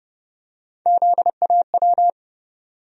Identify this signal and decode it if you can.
Morse code, heard as a keyed tone, about 30 wpm, 700 Hz. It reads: ZAW
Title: USA Call Sign Postfixes - 3 Letters 30wpm